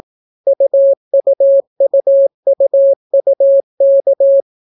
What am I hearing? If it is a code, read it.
Morse code, heard as a keyed tone, about 18 wpm, 550 Hz. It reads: UUUUUK